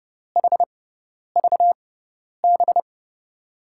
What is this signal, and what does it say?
Morse code, heard as a keyed tone, about 32 wpm, spaced as for 10 wpm, 700 Hz. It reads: HVB
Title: USA Call Signs - 5 Characters - 6x Character Spacing - 30wpm